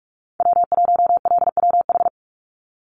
Morse code, heard as a keyed tone, about 44 wpm, 700 Hz. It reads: W1LWH
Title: Call Signs Repeated 3x - 45wpm